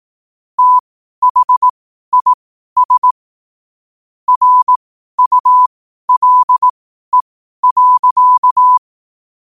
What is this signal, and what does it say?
Morse code, heard as a keyed tone, about 18 wpm, spaced as for 12 wpm, 1000 Hz. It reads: THIS RULE.
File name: qotdfiles / 12wpm